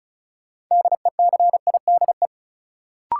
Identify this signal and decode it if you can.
Morse code, heard as a keyed tone, about 34 wpm, 700 Hz. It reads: DECIDE
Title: Sentences from Top 500 Words - New Word Drill - 35wpm